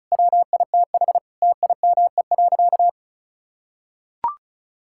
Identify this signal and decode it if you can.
Morse code, heard as a keyed tone, about 34 wpm, 700 Hz. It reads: WITH TIME.
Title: Sentences from Top 1000 Words - Review Part 5 - 35wpm